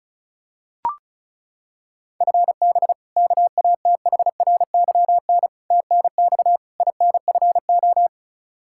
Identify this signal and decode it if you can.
Morse code, heard as a keyed tone, about 36 wpm, 700 Hz. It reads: FB KATHRYN TNX INFO
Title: Practice QSO - Exchange 3 - More Info - 35wpm